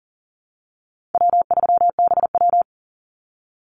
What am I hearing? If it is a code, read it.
Morse code, heard as a keyed tone, about 40 wpm, 700 Hz. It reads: W3BW
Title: Easy Call Signs - 40wpm